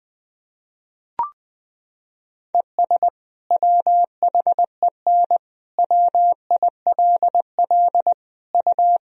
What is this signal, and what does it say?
Morse code, heard as a keyed tone, about 20 wpm, 700 Hz. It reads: ES WHEN WILL U
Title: Sentences from Top 100 Words and Abbreviations 20wpm